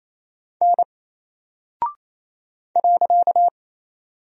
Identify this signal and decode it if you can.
Morse code, heard as a keyed tone, about 28 wpm, 700 Hz. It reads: N .
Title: Single Character (Letter:Number:Punctuation:Prosign) Alt 28wpm